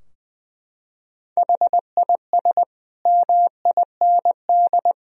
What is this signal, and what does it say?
Morse code, heard as a keyed tone, about 20 wpm, 700 Hz. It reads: HIS MIND
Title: Sentences from Top 1000 Words - Review Part 4 - 20wpm